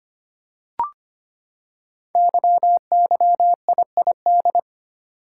Morse code, heard as a keyed tone, about 25 wpm, 700 Hz. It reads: YYIID
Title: Code Groups - 5 Letter:Numbers - 25wpm